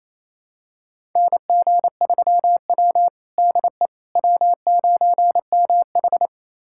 Morse code, heard as a keyed tone, about 29 wpm, 690 Hz. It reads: NG3W DE W9MH
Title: USA Callsign DE Callsign - 4 Characters - 28wpm